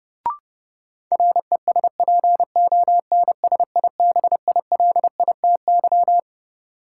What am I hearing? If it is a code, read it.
Morse code, heard as a keyed tone, about 31 wpm, 700 Hz. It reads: RESPONSIBILITY
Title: Sentences from Top 1200 Words - New Word Drill Alt - 30wpm